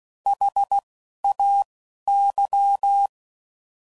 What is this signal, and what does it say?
Morse code, heard as a keyed tone, about 16 wpm, spaced as for 11 wpm, 800 Hz. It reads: HAY
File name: Prefixes / By First Letter / S vs H